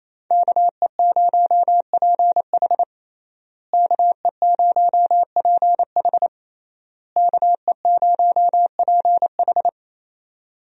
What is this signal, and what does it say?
Morse code, heard as a keyed tone, about 28 wpm, 700 Hz. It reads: KE0PH KE0PH KE0PH
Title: USA Call Signs - 5 Characters - Repeated 3x - 28wpm